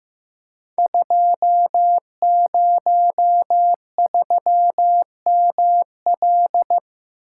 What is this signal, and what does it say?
Morse code, heard as a keyed tone, about 15 wpm, 700 Hz. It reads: 203ML
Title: Code Groups - 5 Letter:Numbers - 15wpm